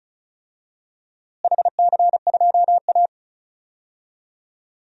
Morse code, heard as a keyed tone, about 36 wpm, 700 Hz. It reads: HC2A
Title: DXCC-100 Call Signs - 4 Characters - Repeated 3x - 35wpm